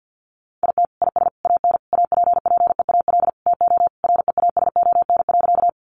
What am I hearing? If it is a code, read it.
Morse code, heard as a keyed tone, about 50 wpm, 700 Hz. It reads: IT IS AN APPEAL TO REASON.
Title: Sentences from Top 1300 Words 50wpm